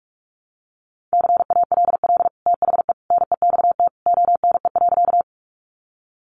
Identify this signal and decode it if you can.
Morse code, heard as a keyed tone, about 44 wpm, 700 Hz. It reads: CALL THE NEXT ONE.
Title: Sentences from Top 300 Words 45wpm